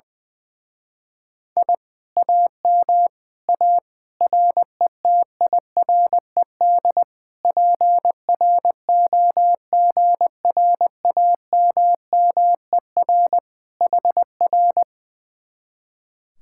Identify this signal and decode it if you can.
Morse code, heard as a keyed tone, about 21 wpm, 700 Hz. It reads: I AM A RETIRED PROGRAMMER HR
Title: Speed-Racing Common QSO Phrases - 20wpm